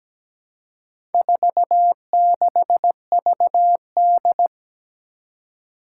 Morse code, heard as a keyed tone, about 17 wpm, 700 Hz. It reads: 46VD
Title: Four Challenging Letter-Number Combinations 17wpm